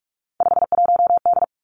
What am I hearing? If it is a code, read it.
Morse code, heard as a keyed tone, about 46 wpm, 700 Hz. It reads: 51D
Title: Three Challenging Letter-Number Combinations 45wpm